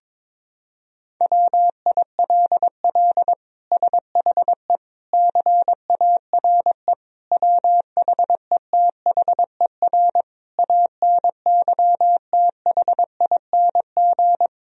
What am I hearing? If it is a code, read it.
Morse code, heard as a keyed tone, about 22 wpm, 700 Hz. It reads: WILL SHE CARE WHETHER ANYTHING
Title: Sentences from Top 1000 Words - Review Part 3 - 22wpm